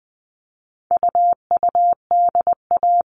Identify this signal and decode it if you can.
Morse code, heard as a keyed tone, about 20 wpm, 700 Hz. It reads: UUDA
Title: All Cut Numbers - 4-Digit 20wpm